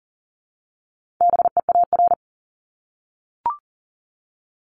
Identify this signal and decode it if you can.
Morse code, heard as a keyed tone, about 39 wpm, 700 Hz. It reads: BEAR
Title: ING Suffix - Top 100 Words with Base Words 40wpm